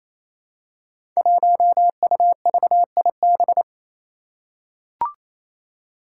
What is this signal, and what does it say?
Morse code, heard as a keyed tone, about 28 wpm, 700 Hz. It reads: 1UVIB
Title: Five Challenging Letter-Number Combinations 28wpm